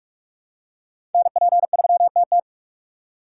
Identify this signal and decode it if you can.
Morse code, heard as a keyed tone, about 44 wpm, 700 Hz. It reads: NP3TT
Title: USA Call Signs - 6x Character Spacing - 45wpm